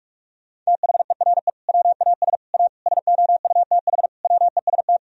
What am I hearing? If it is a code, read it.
Morse code, heard as a keyed tone, about 42 wpm, 700 Hz. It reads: THERE WAS A SOUTH WEST
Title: Sentences from Top 1000 Words - Review Part 2 - 45wpm